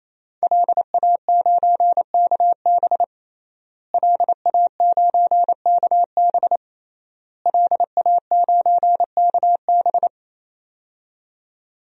Morse code, heard as a keyed tone, about 28 wpm, 700 Hz. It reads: LA9KB LA9KB LA9KB
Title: DXCC-100 Call Signs - 5 Characters - Repeated 3x - 28wpm